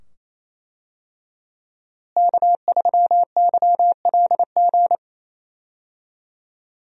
Morse code, heard as a keyed tone, about 28 wpm, 700 Hz. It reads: K3YLG
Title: USA Call Signs - 5 Characters - Repeated 2x - 28wpm